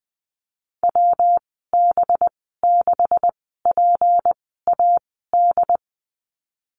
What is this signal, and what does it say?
Morse code, heard as a keyed tone, about 20 wpm, spaced as for 15 wpm, 700 Hz. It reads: WB6PAD
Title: 186 Lesson 30g - Letter b - Call Signs 20wpm